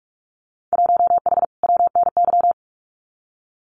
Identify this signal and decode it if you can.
Morse code, heard as a keyed tone, about 46 wpm, 700 Hz. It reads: 1H WNY
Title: Winter Field Day Exchanges 45wpm